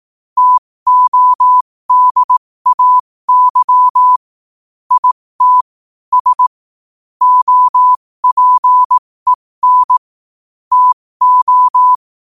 Morse code, heard as a keyed tone, about 18 wpm, spaced as for 15 wpm, 1000 Hz. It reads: TODAY IT S OPEN TO
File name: qotdfiles / 15wpm